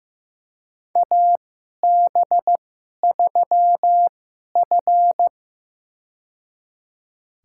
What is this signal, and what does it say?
Morse code, heard as a keyed tone, about 15 wpm, spaced as for 11 wpm, 700 Hz. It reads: AB3F
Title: USA Call Signs - 4 Characters - 2x Character Spacing - 15wpm